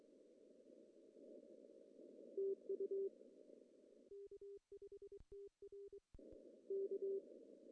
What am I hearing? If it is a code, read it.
Morse code, heard as a keyed tone, about 24 wpm, 400 Hz. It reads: TU K5TR K
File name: CW / cqww-cw-k5tr